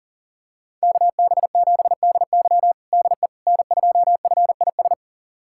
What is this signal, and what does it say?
Morse code, heard as a keyed tone, about 40 wpm, 700 Hz. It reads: KB7DY DE N2FIS
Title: USA Callsign DE Callsign - 5 Characters - 40wpm